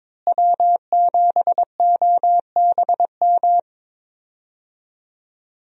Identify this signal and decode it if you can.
Morse code, heard as a keyed tone, about 22 wpm, 700 Hz. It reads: W7OBM
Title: USA Call Signs - 5 Characters - Repeated 2x - 22wpm